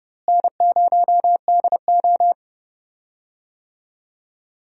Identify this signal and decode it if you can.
Morse code, heard as a keyed tone, about 31 wpm, 700 Hz. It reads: N0DO
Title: USA Call Signs - 4 Characters - Repeated 2x - 30wpm